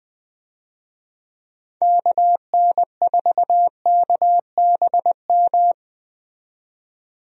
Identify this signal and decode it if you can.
Morse code, heard as a keyed tone, about 20 wpm, 700 Hz. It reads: KN4KBM
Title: USA Call Signs - 6 Characters - 20wpm